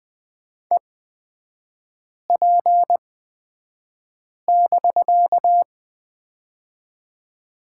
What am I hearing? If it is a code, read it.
Morse code, heard as a keyed tone, about 20 wpm, spaced as for 5 wpm, 700 Hz. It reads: EP<BK>
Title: 255 Lesson 41e - Period - Three Characters 20wpm